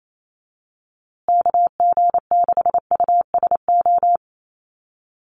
Morse code, heard as a keyed tone, about 28 wpm, 700 Hz. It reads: KG6USO
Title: USA Call Signs Repeated 3x 28wpm